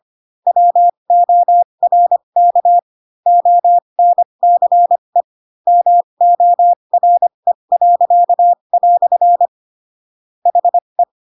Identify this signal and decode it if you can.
Morse code, heard as a keyed tone, about 25 wpm, spaced as for 22 wpm, 700 Hz. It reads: WORK ONCE MORE." HE